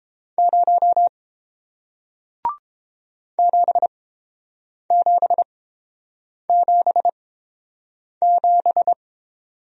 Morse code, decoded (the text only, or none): 0 7777